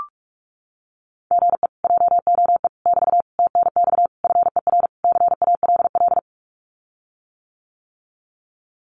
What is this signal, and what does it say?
Morse code, heard as a keyed tone, about 44 wpm, 700 Hz. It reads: GE JOE <BT> TNX FER CALL
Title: Common QSO Phrases Encore 45wpm